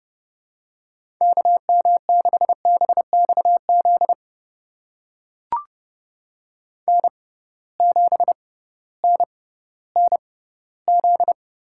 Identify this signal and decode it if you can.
Morse code, heard as a keyed tone, about 32 wpm, spaced as for 17 wpm, 700 Hz. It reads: KM6BXZ N7NNZ
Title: USA Call Signs - 6x Character Spacing - 30wpm